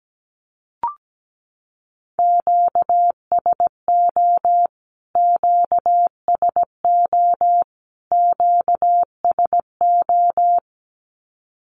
Morse code, heard as a keyed tone, about 17 wpm, 700 Hz. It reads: QSO QSO QSO
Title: Q-Codes Repeated 3x 17wpm